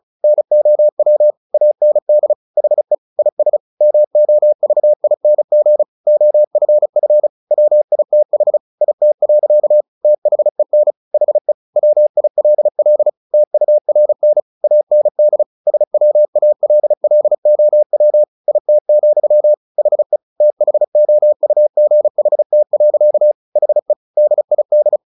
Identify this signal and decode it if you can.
Morse code, heard as a keyed tone, about 34 wpm, 600 Hz. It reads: NOW AND HE IS MOVING OFF WITH IT. THEN HE WILL TURN AND SWALLOW IT, HE THOUGHT. HE DID
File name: the old man and the sea / the old man and the sea 1 35wpm